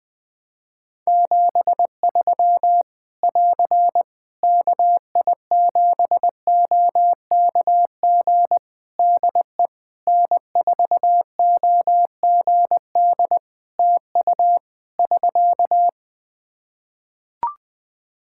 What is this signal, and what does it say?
Morse code, heard as a keyed tone, about 21 wpm, 700 Hz. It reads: 73 <AR> KI7OKG DE N4OGD TU <SK>
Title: K1USN SST QSO Practice 20wpm